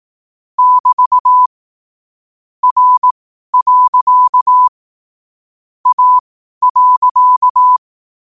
Morse code, heard as a keyed tone, about 18 wpm, spaced as for 12 wpm, 1000 Hz. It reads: <BT> R. A.
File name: qotdfiles / 12wpm